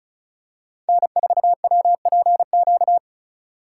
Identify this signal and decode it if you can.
Morse code, heard as a keyed tone, about 36 wpm, 700 Hz. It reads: N4WPQ
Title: USA Call Signs - 5 Characters - 4x Character Spacing - 35wpm